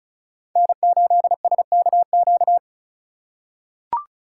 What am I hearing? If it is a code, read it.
Morse code, heard as a keyed tone, about 36 wpm, 700 Hz. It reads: N8SKQ
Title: USA Call Signs - 5 Characters - 4x Character Spacing - 35wpm